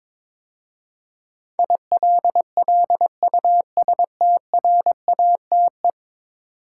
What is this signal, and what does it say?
Morse code, heard as a keyed tone, about 22 wpm, 700 Hz. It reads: ILLUSTRATE
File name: practice / 22wpm / 10 Letter Words - First 3 Letters Given - 22wpm